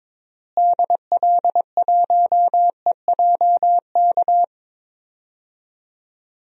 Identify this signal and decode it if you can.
Morse code, heard as a keyed tone, about 22 wpm, 700 Hz. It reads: DL1EJK
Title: DXCC-100 Call Signs - 6 Characters - Repeated 3x - 22wpm